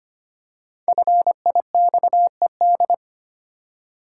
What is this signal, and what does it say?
Morse code, heard as a keyed tone, about 25 wpm, 700 Hz. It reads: FIXED
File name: 5 Letter Words - First 3 Letters Given - 25wpm